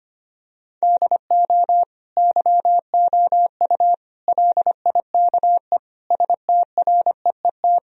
Morse code, heard as a keyed tone, about 25 wpm, 700 Hz. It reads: DO YOU LIKE STREET
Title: Sentences from Top 1000 Words - Review Part 3 - 25wpm